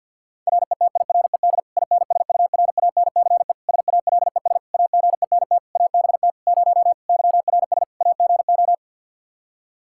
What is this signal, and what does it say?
Morse code, heard as a keyed tone, about 52 wpm, 700 Hz. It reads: RETIRED INSURANCE SALES AGENT ABT 0 YRS AGO
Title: Practice QSO - Exchange 3 - More Info - 50wpm